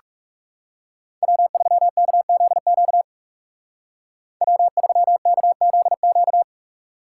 Morse code, heard as a keyed tone, about 37 wpm, 700 Hz. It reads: W3KZQ W3KZQ
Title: Speed Racing - USA 1x3 Call Signs 30wpm